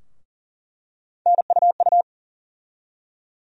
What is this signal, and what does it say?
Morse code, heard as a keyed tone, about 40 wpm, 700 Hz. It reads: NUU
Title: USA Call Sign Postfixes - 3 Letters 40wpm